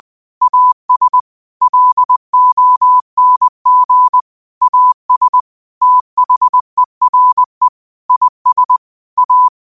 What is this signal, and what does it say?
Morse code, heard as a keyed tone, about 20 wpm, 1000 Hz. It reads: AS LONG AS THERE IS A
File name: qotdfiles / 20wpm